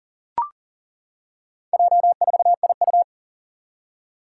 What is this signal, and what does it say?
Morse code, heard as a keyed tone, about 39 wpm, 700 Hz. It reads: J4IU
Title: Four Challenging Letter-Number Combinations 40wpm